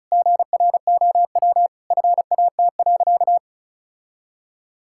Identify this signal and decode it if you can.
Morse code, heard as a keyed tone, about 36 wpm, 700 Hz. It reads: GROW FAT.
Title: Sentences from Top 1000 Words - Review Part 5 - 35wpm